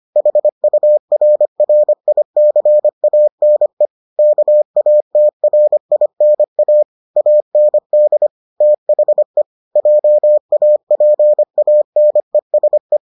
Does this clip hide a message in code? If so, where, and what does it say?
Yes: Morse code from 0.2 to 13.0 s, heard as a keyed tone, about 25 wpm, 600 Hz. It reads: HURRICANE KATRINA AND THE JAPANESE